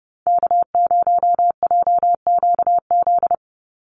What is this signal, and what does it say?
Morse code, heard as a keyed tone, about 30 wpm, 700 Hz. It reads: K0JQZ
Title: NA SOTA Chaser Callsigns 30wpm